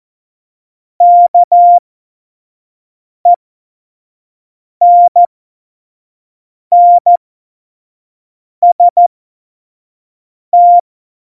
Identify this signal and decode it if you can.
Morse code, heard as a keyed tone, about 14 wpm, spaced as for 5 wpm, 700 Hz. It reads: KENNST
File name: Morsekurs / Witze / Witz 17. November 2020, 25 F70